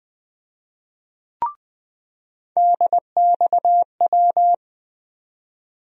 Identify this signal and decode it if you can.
Morse code, heard as a keyed tone, about 21 wpm, 700 Hz. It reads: DXW
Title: USA Call Sign Postfixes - 3 Letters 20wpm